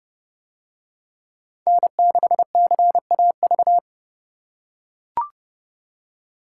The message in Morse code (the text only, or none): N6CAV